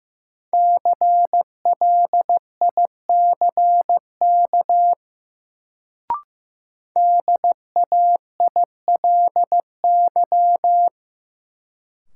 Morse code, heard as a keyed tone, about 15 wpm, 700 Hz. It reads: CLICK DAILY
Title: Sentences from Top 1100 Words - New Word Drill - 15wpm